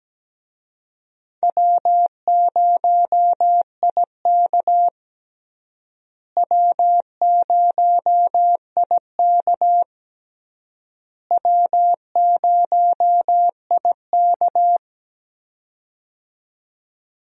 Morse code, heard as a keyed tone, about 17 wpm, 700 Hz. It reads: W0IK W0IK W0IK
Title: USA Call Signs - 4 Characters - Repeated 3x - 17wpm